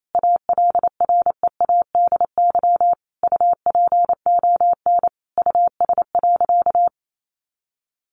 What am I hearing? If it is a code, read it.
Morse code, heard as a keyed tone, about 28 wpm, 700 Hz. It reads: ALREADY UPON US.
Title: Sentences from Top 1500 Words 28wpm